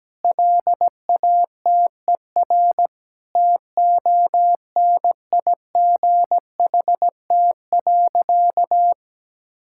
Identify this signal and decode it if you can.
Morse code, heard as a keyed tone, about 17 wpm, 700 Hz. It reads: LATER TONIGHT.
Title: Sentences from Top 1000 Words - Review Part 6 - 17wpm